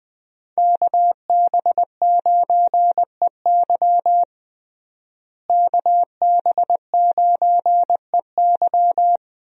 Morse code, heard as a keyed tone, about 20 wpm, 700 Hz. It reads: KB9EY KB9EY
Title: USA Call Signs - 5 Characters - Repeated 2x - 20wpm